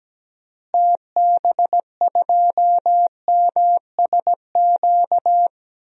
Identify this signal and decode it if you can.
Morse code, heard as a keyed tone, about 17 wpm, 700 Hz. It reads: TB2MSQ
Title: DXCC-100 Call Signs - 6 Characters - Repeated 3x - 17wpm